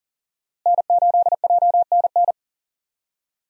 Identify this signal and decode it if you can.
Morse code, heard as a keyed tone, about 40 wpm, 700 Hz. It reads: N8JNN
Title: USA Call Signs - 5 Characters - Repeated 3x - 40wpm